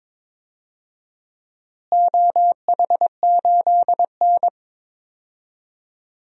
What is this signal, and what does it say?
Morse code, heard as a keyed tone, about 22 wpm, 700 Hz. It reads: OH8N
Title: DXCC-100 Call Signs - 4 Characters - Repeated 3x - 22wpm